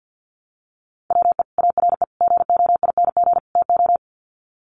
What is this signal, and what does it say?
Morse code, heard as a keyed tone, about 50 wpm, 700 Hz. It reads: WE ARE GOING TO
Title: Sets of 4 Words 50wpm